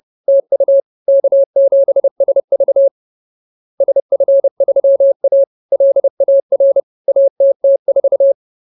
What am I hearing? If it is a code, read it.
Morse code, heard as a keyed tone, about 30 wpm, 550 Hz. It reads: TU K7SV SF3A LAR ATT4
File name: CWT-216-30